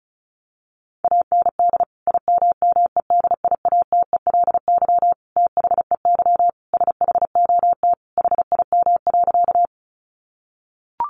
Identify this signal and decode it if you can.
Morse code, heard as a keyed tone, about 34 wpm, 700 Hz. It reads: AND IMMEDIATELY THEY SHOT HIM.
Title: Sentences from Top 1100 Words 35wpm